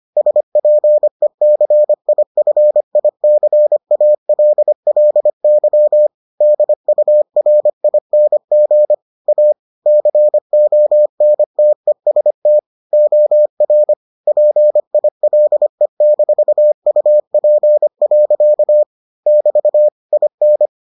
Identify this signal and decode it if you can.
Morse code, heard as a keyed tone, about 25 wpm, 600 Hz. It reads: SPECIFICALLY DURING A CONTEST OR PILE-UP. <BT> IN